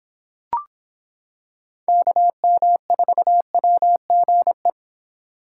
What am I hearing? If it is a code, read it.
Morse code, heard as a keyed tone, about 26 wpm, 700 Hz. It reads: KM4WGE